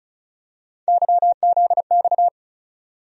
Morse code, heard as a keyed tone, about 36 wpm, 700 Hz. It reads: YZX
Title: USA Call Sign Postfixes - 3 Letters 35wpm